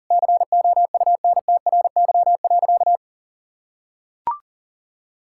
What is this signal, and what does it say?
Morse code, heard as a keyed tone, about 37 wpm, 700 Hz. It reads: COUNTRY.
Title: Sentences from Top 1000 Words - Review Part 5 - 40wpm